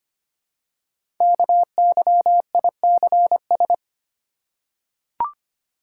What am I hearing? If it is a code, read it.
Morse code, heard as a keyed tone, about 25 wpm, 700 Hz. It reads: KYICS
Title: Code Groups - 5 Letters - 25wpm